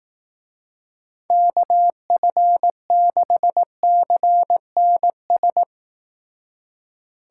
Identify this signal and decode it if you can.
Morse code, heard as a keyed tone, about 18 wpm, 700 Hz. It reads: KF6CNS
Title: Speed Racing - USA 2x3 Call Signs 15wpm